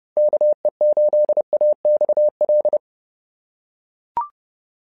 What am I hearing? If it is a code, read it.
Morse code, heard as a keyed tone, about 30 wpm, 590 Hz. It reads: KE8AXL
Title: SOTA QSO Practice 30wpm